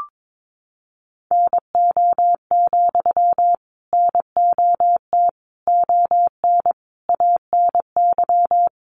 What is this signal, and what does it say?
Morse code, heard as a keyed tone, about 22 wpm, 700 Hz. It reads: NO, NOT ON ANY
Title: Sentences from Top 900 Words 22wpm